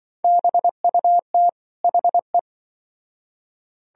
Morse code, heard as a keyed tone, about 24 wpm, 700 Hz. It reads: BUT HE
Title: Speed-Racing - Sets of 2 Words 20wpm